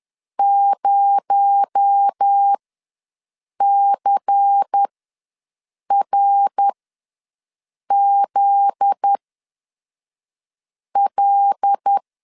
Morse code, heard as a keyed tone, about 10 wpm, spaced as for 5 wpm, 800 Hz. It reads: 0CRZL
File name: LEZIONI / 7 GF9Z / 7 GF9Z ws 10wpm 3919